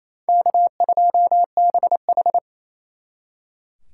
Morse code, heard as a keyed tone, about 35 wpm, 700 Hz. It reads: K2BH
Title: Speed Racing - USA 1x2 Call Signs 28wpm